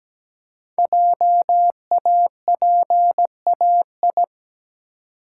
Morse code, heard as a keyed tone, about 17 wpm, 700 Hz. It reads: JAPAI
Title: Code Groups - 5 Letter:Numbers - 17wpm